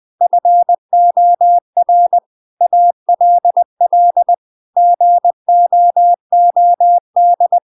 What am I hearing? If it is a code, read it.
Morse code, heard as a keyed tone, about 20 wpm, 700 Hz. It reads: FOR ALL GOOD
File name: Now is the time - 20wpm